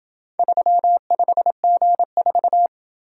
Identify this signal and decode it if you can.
Morse code, heard as a keyed tone, about 27 wpm, 700 Hz. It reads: 35G4